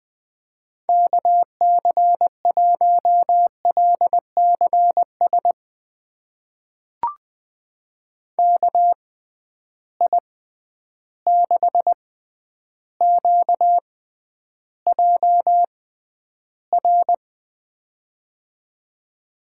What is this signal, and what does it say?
Morse code, heard as a keyed tone, about 20 wpm, spaced as for 10 wpm, 700 Hz. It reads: KC1LCS KI6QJR
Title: USA Call Signs - 6 Characters - 6x Character Spacing - 20wpm